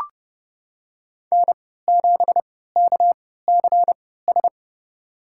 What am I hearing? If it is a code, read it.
Morse code, heard as a keyed tone, about 30 wpm, spaced as for 17 wpm, 700 Hz. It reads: N7KCS
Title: USA Call Signs - 5 Characters - 3x Character Spacing - 30wpm